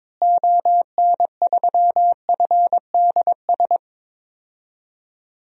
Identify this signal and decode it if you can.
Morse code, heard as a keyed tone, about 22 wpm, 700 Hz. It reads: ON3FDS
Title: DXCC-100 Call Signs - 6 Characters - Repeated 2x - 22wpm